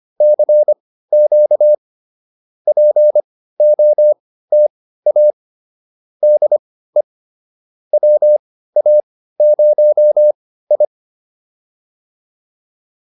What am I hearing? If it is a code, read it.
Morse code, heard as a keyed tone, about 25 wpm, spaced as for 15 wpm, 590 Hz. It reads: CQ POTA DE WA0I